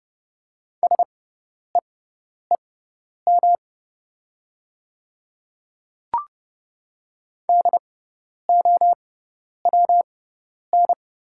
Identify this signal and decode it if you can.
Morse code, heard as a keyed tone, about 30 wpm, spaced as for 10 wpm, 700 Hz. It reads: SEEM DOWN
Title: 6x Character Spacing - Top 300 Words - 30wpm